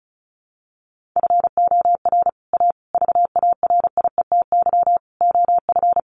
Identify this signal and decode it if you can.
Morse code, heard as a keyed tone, about 34 wpm, 700 Hz. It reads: FOR A VARIETY OF
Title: Sets of 5 Words Encore 35wpm